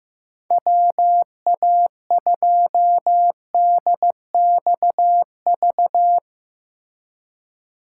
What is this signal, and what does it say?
Morse code, heard as a keyed tone, about 17 wpm, 700 Hz. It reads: WA2DXV
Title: Speed Racing - USA 2x3 Call Signs 15wpm